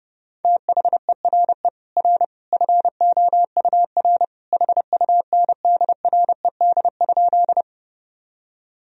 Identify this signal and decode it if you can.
Morse code, heard as a keyed tone, about 30 wpm, 700 Hz. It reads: THERE R FOUR HUNDRED?
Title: Sentences from Top 400 Words and Abbreviations 30wpm